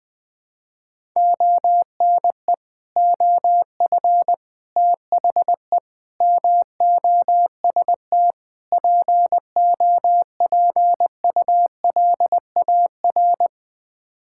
Morse code, heard as a keyed tone, about 21 wpm, 700 Hz. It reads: ONE OF THE MOST POPULAR
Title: Build A Phrase - 5-Word Phrases - 20wpm